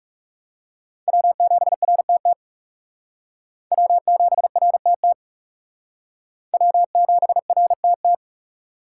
Morse code, heard as a keyed tone, about 37 wpm, 700 Hz. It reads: W7RTT W7RTT W7RTT
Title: Speed Racing - USA 1x3 Call Signs 30wpm